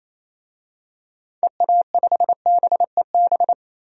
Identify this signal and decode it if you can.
Morse code, heard as a keyed tone, about 27 wpm, 700 Hz. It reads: EA5BEB
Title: DXCC-100 Call Signs - 6 Characters - 28wpm